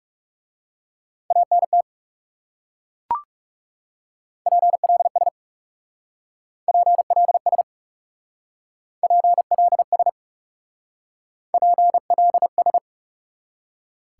Morse code, heard as a keyed tone, about 36 wpm, 700 Hz. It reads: ANT PLS PLS PLS PLS
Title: Speed-Racing Top 90 QSO Elements 30wpm